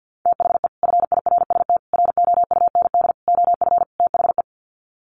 Morse code, heard as a keyed tone, about 50 wpm, 700 Hz. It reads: THE FIRST ROUND OF THE
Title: Sets of 5 Words Encore 50wpm